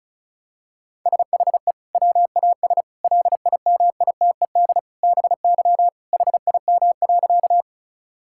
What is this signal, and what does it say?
Morse code, heard as a keyed tone, about 34 wpm, 700 Hz. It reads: SHE WAS LIMITED BY HIM.
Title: Sentences from Top 1000 Words - Review Part 6 - 35wpm